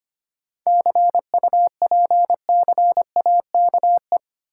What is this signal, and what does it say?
Morse code, heard as a keyed tone, about 25 wpm, 700 Hz. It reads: CUPCAKE
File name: practice / 25wpm / Difficult Words with Hints - 25wpm